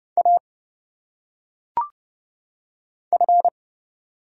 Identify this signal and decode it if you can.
Morse code, heard as a keyed tone, about 29 wpm, 700 Hz. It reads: A F
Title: Single Letters - Phonetic Pronunciation 30wpm